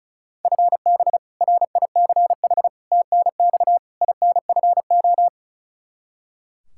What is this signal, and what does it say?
Morse code, heard as a keyed tone, about 36 wpm, 700 Hz. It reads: FB RICH TNX INFO
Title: Practice QSO - Exchange 3 - More Info - 35wpm